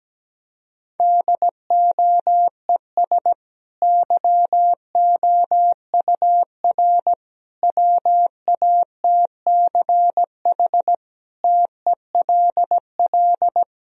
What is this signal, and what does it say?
Morse code, heard as a keyed tone, about 17 wpm, 700 Hz. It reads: DOES YOUR WATCH TELL
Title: Sentences from Top 1000 Words - Review Part 1 - 17wpm